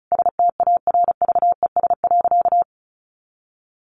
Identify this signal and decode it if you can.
Morse code, heard as a keyed tone, about 34 wpm, 700 Hz. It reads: STARVES.
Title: A Sound Of Thunder 35wpm